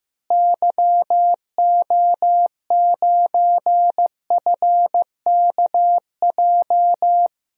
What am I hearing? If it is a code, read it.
Morse code, heard as a keyed tone, about 15 wpm, 700 Hz. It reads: YO9FKJ
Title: DXCC-100 Call Signs - 6 Characters - Repeated 2x - 15wpm